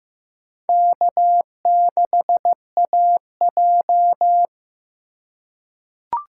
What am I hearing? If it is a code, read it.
Morse code, heard as a keyed tone, about 15 wpm, 700 Hz. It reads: K6AJ
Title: USA Call Signs - 4 Characters - 4x Character Spacing - 15wpm